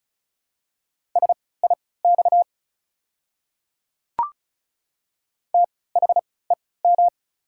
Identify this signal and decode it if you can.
Morse code, heard as a keyed tone, about 36 wpm, spaced as for 20 wpm, 690 Hz. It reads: SIX THEM
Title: 3x Character Spacing - Top 500 Words - 35wpm